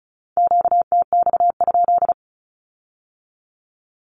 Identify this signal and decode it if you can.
Morse code, heard as a keyed tone, about 36 wpm, 700 Hz. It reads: QTX?
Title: Q-Codes with Spoken Meaning - 35wpm